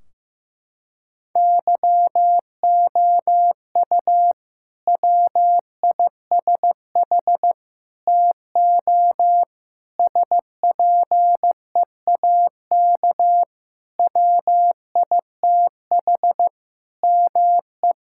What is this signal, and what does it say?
Morse code, heard as a keyed tone, about 15 wpm, 700 Hz. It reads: YOU WISH TO SPEAK WITH ME
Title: Sentences from Top 1600 Words 15wpm